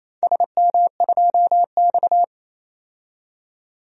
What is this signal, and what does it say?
Morse code, heard as a keyed tone, about 27 wpm, 690 Hz. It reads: SM2X
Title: DXCC-100 Call Signs - 4 Characters - 28wpm